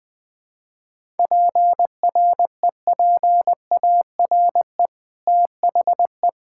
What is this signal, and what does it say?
Morse code, heard as a keyed tone, about 20 wpm, 700 Hz. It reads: PREPARE THE
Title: Sentences from Top 1000 Words - Review Part 6 - 20wpm